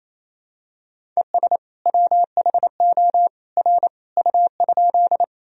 Morse code, heard as a keyed tone, about 28 wpm, 700 Hz. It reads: ES WHO R U?
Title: Sentences from Top 100 Words and Abbreviations 28wpm